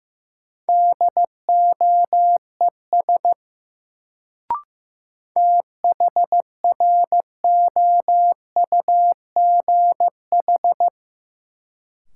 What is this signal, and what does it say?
Morse code, heard as a keyed tone, about 15 wpm, 700 Hz. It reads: DOES THROUGH
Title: Sentences from Top 200 Words - New Word Drill - 15wpm